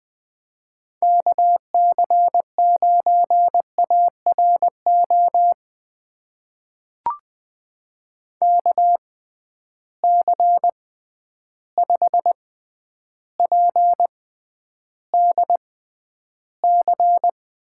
Morse code, heard as a keyed tone, about 20 wpm, spaced as for 10 wpm, 700 Hz. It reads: KC9ARO KC5PDC
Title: USA Call Signs - 6 Characters - 6x Character Spacing - 20wpm